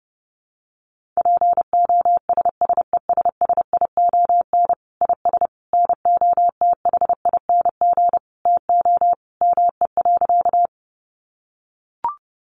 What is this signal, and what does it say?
Morse code, heard as a keyed tone, about 30 wpm, 700 Hz. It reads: POSSESSION IS NOTHING TO ME.
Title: Sentences from Top 1400 Words 30wpm